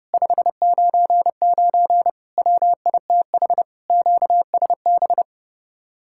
Morse code, heard as a keyed tone, about 30 wpm, 700 Hz. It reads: 599 WITH QSB